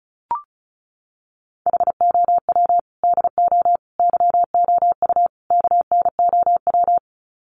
Morse code, heard as a keyed tone, about 34 wpm, 700 Hz. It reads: HOW DO YOU KNOW
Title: Sets of 4 Words 35wpm